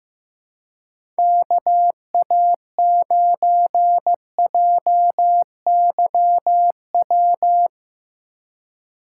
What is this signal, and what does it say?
Morse code, heard as a keyed tone, about 16 wpm, 700 Hz. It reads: KA9JYW
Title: USA Call Signs - 6 Characters - 3x Character Spacing - 15wpm